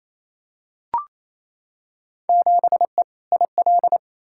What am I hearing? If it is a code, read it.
Morse code, heard as a keyed tone, about 28 wpm, 700 Hz. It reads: 7E IL
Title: Field Day Exchanges Encore - 28wpm